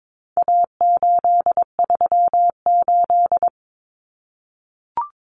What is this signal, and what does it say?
Morse code, heard as a keyed tone, about 22 wpm, 700 Hz. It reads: A838
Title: Common Cut Numbers - 4-Digit 22wpm